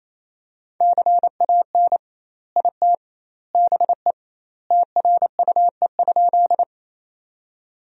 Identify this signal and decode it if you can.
Morse code, heard as a keyed tone, about 28 wpm, 700 Hz. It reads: CAN IT BE TRUE?
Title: 2x Word-Spacing - Sentences from Top 400 Words - 28wpm